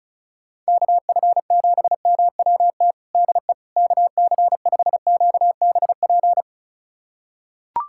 Morse code, heard as a keyed tone, about 36 wpm, 700 Hz. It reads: KF7MWT DE KC5QBP
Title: USA Callsign DE Callsign - 6 Characters - 35wpm